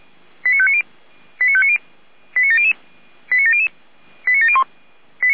Five-tone selective call (ZVEI), heard as quick runs of five tones, tones of about 70 ms. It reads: 89590 89590 89700 89700 89701